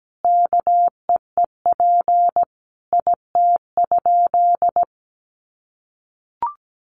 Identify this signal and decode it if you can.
Morse code, heard as a keyed tone, about 17 wpm, 700 Hz. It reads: KEEP IT?
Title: Sentences from Top 300 Words 17wpm